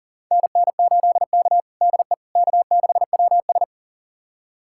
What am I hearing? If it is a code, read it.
Morse code, heard as a keyed tone, about 40 wpm, 700 Hz. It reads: NN8K DE K6WS
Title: USA Callsign DE Callsign - 4 Characters - 40wpm